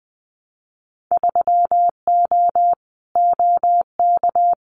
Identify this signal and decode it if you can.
Morse code, heard as a keyed tone, about 20 wpm, 700 Hz. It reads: 3O OK
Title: Winter Field Day Exchanges Encore 20wpm